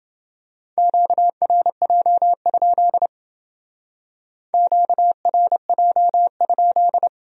Morse code, heard as a keyed tone, about 28 wpm, 700 Hz. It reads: QRJ? QRJ?